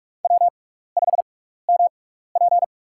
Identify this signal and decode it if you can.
Morse code, heard as a keyed tone, about 44 wpm, spaced as for 15 wpm, 700 Hz. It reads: W5MP
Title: USA Call Signs - 6x Character Spacing - 45wpm